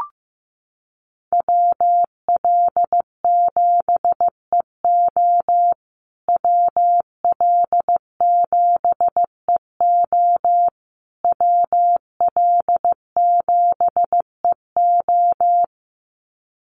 Morse code, heard as a keyed tone, about 15 wpm, 700 Hz. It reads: WL7EO WL7EO WL7EO
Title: USA Call Signs Repeated 3x 15wpm